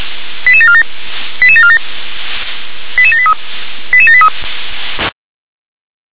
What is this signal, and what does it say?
Five-tone selective call (ZVEI), heard as quick runs of five tones, tones of about 70 ms. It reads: 88747 88747 88783 88783